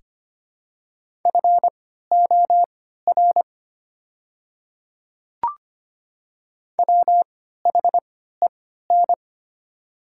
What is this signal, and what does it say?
Morse code, heard as a keyed tone, about 25 wpm, spaced as for 14 wpm, 700 Hz. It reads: FOR WHEN
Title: 3x Character Spacing - Top 100 Words - 25wpm